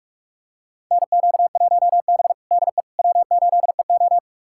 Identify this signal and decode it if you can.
Morse code, heard as a keyed tone, about 41 wpm, 700 Hz. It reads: NQ1B DE W8EO
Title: USA Callsign DE Callsign - 4 Characters - 45wpm